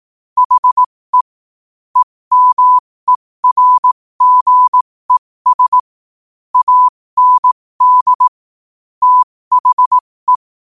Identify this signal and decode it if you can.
Morse code, heard as a keyed tone, about 18 wpm, spaced as for 15 wpm, 1000 Hz. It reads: HE EMERGES AND THE
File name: qotdfiles / 15wpm